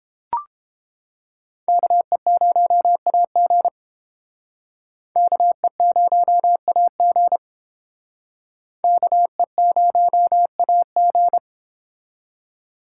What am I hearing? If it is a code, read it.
Morse code, heard as a keyed tone, about 30 wpm, 700 Hz. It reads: KE0AG KE0AG KE0AG